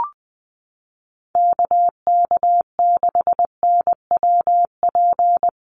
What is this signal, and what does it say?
Morse code, heard as a keyed tone, about 20 wpm, 700 Hz. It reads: KK6NWP
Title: USA Call Signs 20wpm